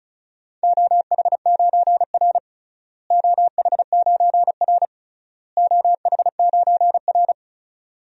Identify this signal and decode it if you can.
Morse code, heard as a keyed tone, about 36 wpm, 700 Hz. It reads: OH9R OH9R OH9R
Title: DXCC-100 Call Signs - 4 Characters - Repeated 3x - 35wpm